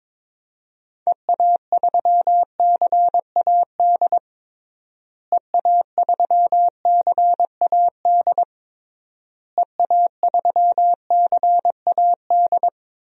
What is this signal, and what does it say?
Morse code, heard as a keyed tone, about 22 wpm, 700 Hz. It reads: EA3CAD EA3CAD EA3CAD
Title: DXCC-100 Call Signs - 6 Characters - Repeated 3x - 22wpm